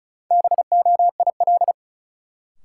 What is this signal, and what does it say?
Morse code, heard as a keyed tone, about 36 wpm, 700 Hz. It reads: BOIL
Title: Sets of 5 Words Encore - New Word Drill - 35wpm